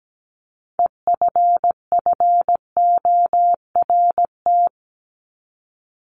Morse code, heard as a keyed tone, about 17 wpm, 700 Hz. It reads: EFFORT
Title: Top 1001-1250 Words 17wpm